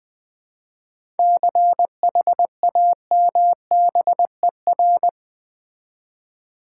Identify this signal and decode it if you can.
Morse code, heard as a keyed tone, about 20 wpm, 700 Hz. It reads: CHAMBER
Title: 7 Letter Words - First 3 Letters Given - 20wpm